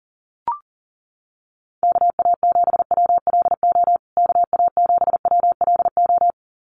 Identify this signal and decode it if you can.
Morse code, heard as a keyed tone, about 40 wpm, 700 Hz. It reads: KA7WLO KA7WLO
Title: USA Call Signs Repeated 2x 40wpm